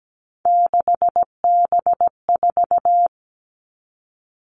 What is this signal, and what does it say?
Morse code, heard as a keyed tone, about 17 wpm, 700 Hz. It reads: 6B4
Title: All Cut Numbers - 3-Digit 17wpm